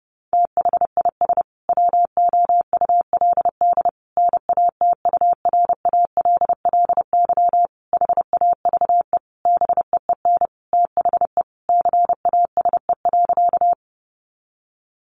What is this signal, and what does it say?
Morse code, heard as a keyed tone, about 28 wpm, 700 Hz. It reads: THIS WOULD NATURALLY HAVE BEEN THE CASE.
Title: Sentences from Top 1300 Words 30wpm